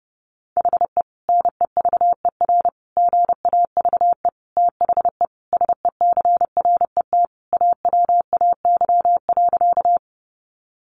Morse code, heard as a keyed tone, about 30 wpm, 700 Hz. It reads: HE NEVER GAVE THE SECRET AWAY.
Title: Sentences from Top 1200 Words 30wpm